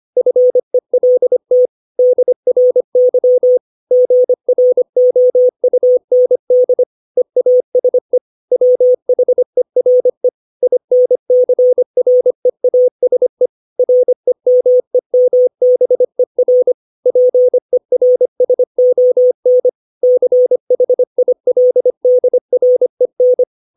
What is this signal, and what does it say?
Morse code, heard as a keyed tone, about 25 wpm, 500 Hz. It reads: FELT DRY GROUND EASE WHERE INCREASE REMEMBER PERSON CHILDREN
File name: 632 words / 632 Words 25WPM